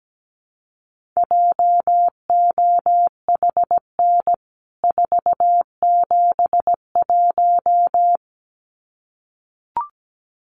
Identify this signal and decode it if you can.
Morse code, heard as a keyed tone, about 17 wpm, 700 Hz. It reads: JOHN 471
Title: CWT Exchange 17wpm